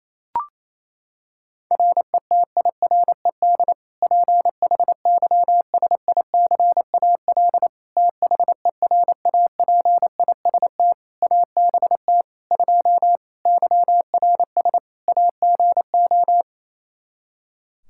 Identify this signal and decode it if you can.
Morse code, heard as a keyed tone, about 27 wpm, 700 Hz. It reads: RETIRED PHYSICAL THERAPIST ABT 2 YRS AGO
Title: K1USN SST QSO Practice 28wpm